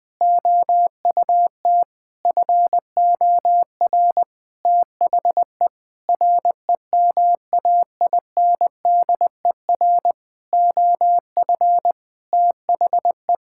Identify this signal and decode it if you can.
Morse code, heard as a keyed tone, about 20 wpm, 700 Hz. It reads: OUT FOR THE REMAINDER OF THE
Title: Sentences with 6-Word Phrases - 20wpm